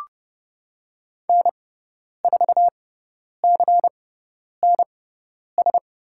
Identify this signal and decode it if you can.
Morse code, heard as a keyed tone, about 30 wpm, spaced as for 10 wpm, 700 Hz. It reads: N4CNS
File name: Lesson 16g - Letter c - Call Signs 30wpm